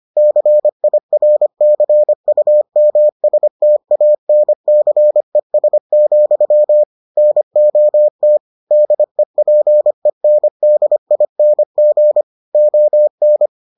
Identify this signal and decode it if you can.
Morse code, heard as a keyed tone, about 25 wpm, 600 Hz. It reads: CIRCUMSTANCES, NOT DEPENDING ON